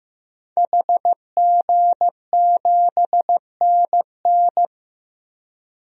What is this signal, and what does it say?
Morse code, heard as a keyed tone, about 15 wpm, 700 Hz. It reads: HG7NN
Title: DXCC-100 Call Signs - 5 Characters - 15wpm